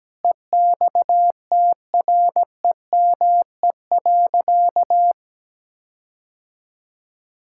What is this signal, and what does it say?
Morse code, heard as a keyed tone, about 17 wpm, 700 Hz. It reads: EXTREME.
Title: Sentences from Top 1600 Words 17wpm